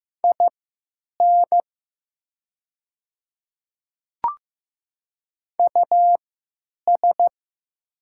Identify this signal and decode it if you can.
Morse code, heard as a keyed tone, about 15 wpm, spaced as for 9 wpm, 700 Hz. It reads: IN US
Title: 3x Character Spacing - Top 300 Words - 15wpm